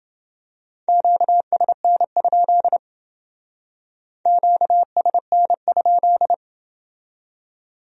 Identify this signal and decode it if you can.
Morse code, heard as a keyed tone, about 28 wpm, 700 Hz. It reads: QSN? QSN?